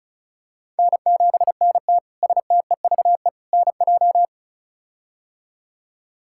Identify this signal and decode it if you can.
Morse code, heard as a keyed tone, about 34 wpm, 700 Hz. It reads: N7NT STEVE NJ
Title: K1USN SST QSO Practice - 35wpm